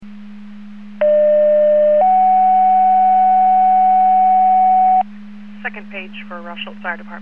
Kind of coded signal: two-tone page